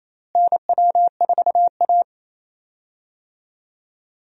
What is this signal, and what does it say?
Morse code, heard as a keyed tone, about 28 wpm, 700 Hz. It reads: NW4A
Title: USA Call Signs - 4 Characters - Repeated 3x - 28wpm